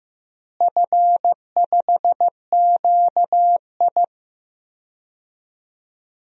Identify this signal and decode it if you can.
Morse code, heard as a keyed tone, about 15 wpm, 700 Hz. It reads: F5QI
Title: DXCC-100 Call Signs - 4 Characters - Repeated 3x - 15wpm